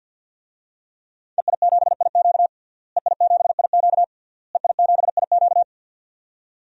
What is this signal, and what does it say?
Morse code, heard as a keyed tone, about 49 wpm, 700 Hz. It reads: EI7IQ EI7IQ EI7IQ
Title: DXCC-100 Call Signs - 5 Characters - Repeated 3x - 50wpm